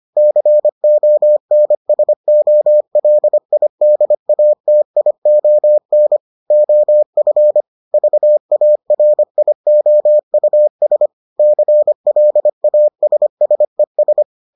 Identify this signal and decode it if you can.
Morse code, heard as a keyed tone, about 25 wpm, 600 Hz. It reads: CONSOLIDATION OF VARIOUS CLASSES